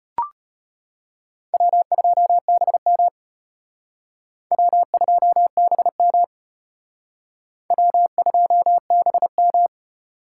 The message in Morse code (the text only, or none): W2BM W2BM W2BM